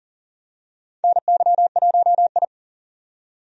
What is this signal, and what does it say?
Morse code, heard as a keyed tone, about 40 wpm, 700 Hz. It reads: NY1I
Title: USA Call Signs - 4 Characters - 40wpm